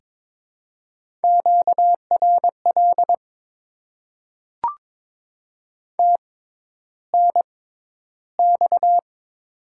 Morse code, heard as a keyed tone, about 22 wpm, spaced as for 11 wpm, 700 Hz. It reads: QRL TNX
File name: Top 20 QSO Elements - 6x Character Spacing - 22wpm